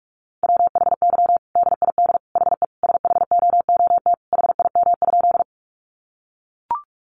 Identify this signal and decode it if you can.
Morse code, heard as a keyed tone, about 44 wpm, 700 Hz. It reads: WHY DID HE SHOOT HIM?
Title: Sentences from Top 1200 Words 45wpm